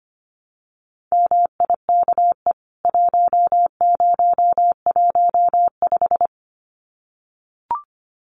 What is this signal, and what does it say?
Morse code, heard as a keyed tone, about 25 wpm, 700 Hz. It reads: MIKE 1015
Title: CWT Exchange 25wpm